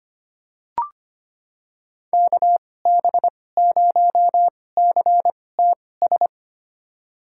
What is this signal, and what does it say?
Morse code, heard as a keyed tone, about 25 wpm, spaced as for 18 wpm, 700 Hz. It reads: KB0CTS
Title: USA Call Signs - 6 Characters - 2x Character Spacing - 25wpm